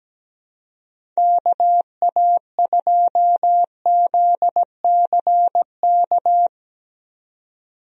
Morse code, heard as a keyed tone, about 17 wpm, 700 Hz. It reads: KA2ZCK
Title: USA Call Signs - 6 Characters - 17wpm